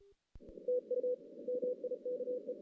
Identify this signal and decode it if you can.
Morse code, heard as a keyed tone, about 33 wpm, 500 Hz. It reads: TU MIKE
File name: 2016 JAN CW NAQP / 2016 Jan naqp k5ot - 0-6